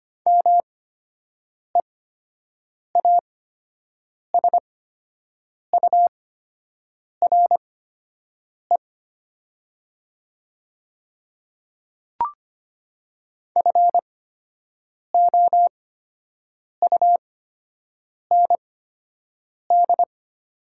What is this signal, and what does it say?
Morse code, heard as a keyed tone, about 25 wpm, spaced as for 7 wpm, 700 Hz. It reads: MEASURE FOUND
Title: 8x Character Spacing - Top 500 Words - 25wpm